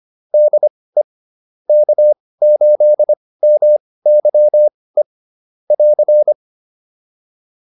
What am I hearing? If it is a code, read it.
Morse code, heard as a keyed tone, about 25 wpm, spaced as for 18 wpm, 600 Hz. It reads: DE K8MYE <AR>
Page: intermediate-practice-files